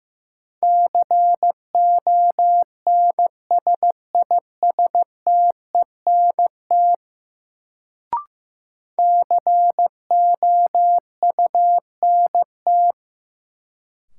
Morse code, heard as a keyed tone, about 15 wpm, 700 Hz. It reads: CONSISTENT COUNT
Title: Sentences from Top 1500 Words - New Word Drill Alt - 15wpm